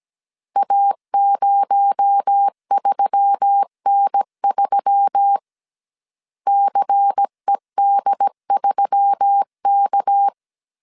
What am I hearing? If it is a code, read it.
Morse code, heard as a keyed tone, about 17 wpm, 800 Hz. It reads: A03N3 CED3K
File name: CW / DET / 1 6 1500C S C 16wpm 2733